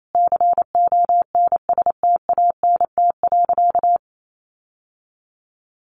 Morse code, heard as a keyed tone, about 28 wpm, 700 Hz. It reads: CONSTANT.
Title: Sentences from Top 1500 Words 28wpm